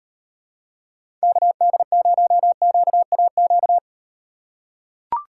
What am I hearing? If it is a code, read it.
Morse code, heard as a keyed tone, about 38 wpm, 700 Hz. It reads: KD0QAQ